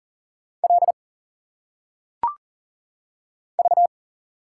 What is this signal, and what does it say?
Morse code, heard as a keyed tone, about 40 wpm, 700 Hz. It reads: L V
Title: Single Letters - Phonetic Pronunciation 40wpm